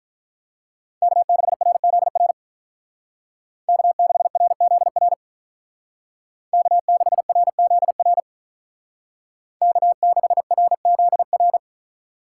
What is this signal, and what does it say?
Morse code, heard as a keyed tone, about 46 wpm, 700 Hz. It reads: K6RZR K6RZR K6RZR K6RZR